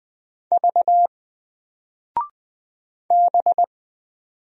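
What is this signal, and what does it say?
Morse code, heard as a keyed tone, about 20 wpm, 700 Hz. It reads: V B
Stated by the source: Single letter and number Morse drill at 20 wpm, from N4LCD.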